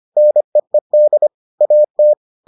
Morse code, heard as a keyed tone, about 25 wpm, 600 Hz. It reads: NEED AT